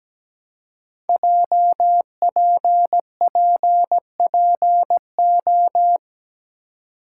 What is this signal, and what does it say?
Morse code, heard as a keyed tone, about 17 wpm, 700 Hz. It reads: JPPPO
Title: Code Groups - 5 Letters - 17wpm